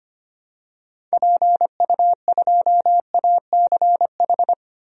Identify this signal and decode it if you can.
Morse code, heard as a keyed tone, about 25 wpm, 700 Hz. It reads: PU2ACH
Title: DXCC-100 Call Signs - 6 Characters - Repeated 3x - 25wpm